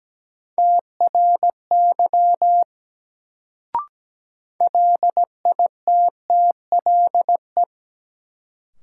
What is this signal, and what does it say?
Morse code, heard as a keyed tone, about 17 wpm, 700 Hz. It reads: TRY LITTLE
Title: Sentences from Top 200 Words - New Word Drill - 17wpm